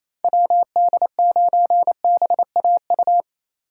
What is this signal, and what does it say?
Morse code, heard as a keyed tone, about 28 wpm, 700 Hz. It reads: WD9BAU
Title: USA Call Signs - 6 Characters - 4x Character Spacing - 28wpm